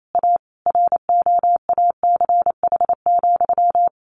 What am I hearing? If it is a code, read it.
Morse code, heard as a keyed tone, about 28 wpm, 700 Hz. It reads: A ROACH,
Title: A Sound Of Thunder 28wpm